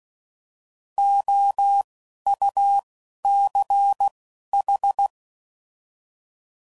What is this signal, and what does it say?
Morse code, heard as a keyed tone, about 16 wpm, spaced as for 11 wpm, 800 Hz. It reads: OUCH
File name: Prefixes / By First Letter / Os